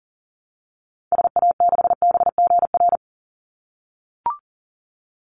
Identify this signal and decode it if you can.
Morse code, heard as a keyed tone, about 40 wpm, 700 Hz. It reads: SA6BGR
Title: Call Signs - 40wpm